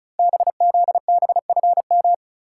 Morse code, heard as a keyed tone, about 36 wpm, 700 Hz. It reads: BZBFM
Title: Code Groups - 5 Letters - 35wpm